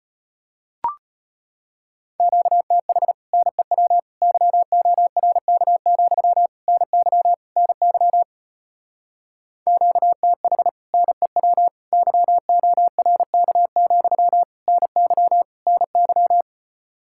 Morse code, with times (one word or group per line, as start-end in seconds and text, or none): 2.2-3.1 QTH
3.3-4.0 NEW
4.2-6.5 YORK,
6.7-7.3 NY
7.6-8.2 NY
9.7-10.7 QTH
10.9-11.7 NEW
11.9-14.4 YORK,
14.7-15.4 NY
15.7-16.4 NY